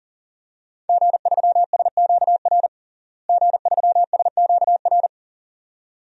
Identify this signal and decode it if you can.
Morse code, heard as a keyed tone, about 37 wpm, 700 Hz. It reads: G3SQR G3SQR
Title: DXCC-100 Call Signs - 5 Characters - Repeated 2x - 40wpm